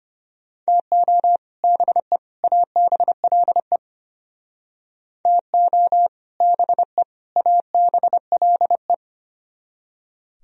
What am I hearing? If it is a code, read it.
Morse code, heard as a keyed tone, about 32 wpm, 700 Hz. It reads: TO BE ABLE TO BE ABLE